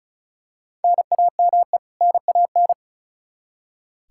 Morse code, heard as a keyed tone, about 36 wpm, 700 Hz. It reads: NAME NAN
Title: Practice QSO - Exchange 1 - Beginning - 35wpm